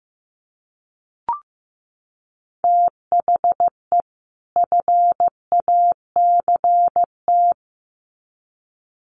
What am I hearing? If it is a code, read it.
Morse code, heard as a keyed tone, about 15 wpm, 700 Hz. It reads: THE FACT
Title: Sets of 2 Words 15wpm